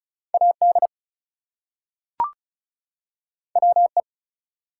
Morse code, heard as a keyed tone, about 36 wpm, 700 Hz. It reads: AD WE
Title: Two Letter Words - 35wpm